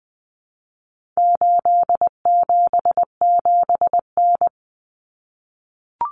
Morse code, heard as a keyed tone, about 20 wpm, 700 Hz. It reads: 877N
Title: Common Cut Numbers - 4-Digit 20wpm